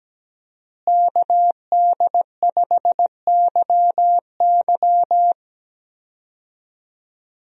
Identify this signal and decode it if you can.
Morse code, heard as a keyed tone, about 17 wpm, 700 Hz. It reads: KD5YY
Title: USA Call Signs - 5 Characters - Repeated 2x - 17wpm